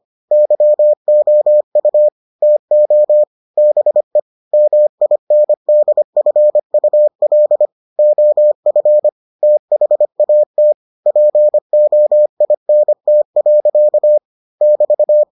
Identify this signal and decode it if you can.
Morse code, heard as a keyed tone, about 25 wpm, 600 Hz. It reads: YOU TO BE MINDFUL OF THAT POINT. <BT>